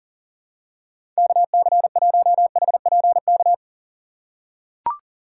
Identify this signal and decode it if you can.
Morse code, heard as a keyed tone, about 40 wpm, 700 Hz. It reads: KC1HPK
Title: USA Call Signs - 6 Characters - 2x Character Spacing - 40wpm